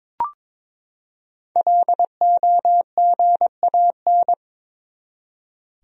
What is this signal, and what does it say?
Morse code, heard as a keyed tone, about 22 wpm, 700 Hz. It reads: LOGAN
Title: USA Common Names - 22wpm